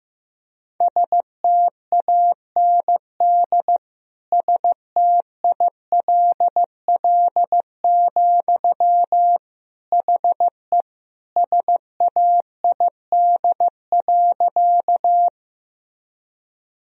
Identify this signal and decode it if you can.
Morse code, heard as a keyed tone, about 15 wpm, 700 Hz. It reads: STAND STILL, HE SAID.
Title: Sentences from Top 1000 Words - Review Part 1 - 15wpm